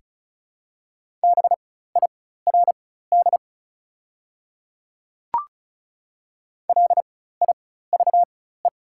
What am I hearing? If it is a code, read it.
Morse code, heard as a keyed tone, about 34 wpm, spaced as for 16 wpm, 690 Hz. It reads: BIRD LIVE
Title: 4x Character Spacing - Top 500 Words - 35wpm